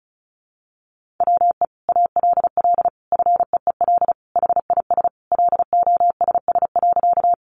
Morse code, heard as a keyed tone, about 34 wpm, 700 Hz. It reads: WE ALL FEEL HIS LOSS.
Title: Sentences from Top 1400 Words 35wpm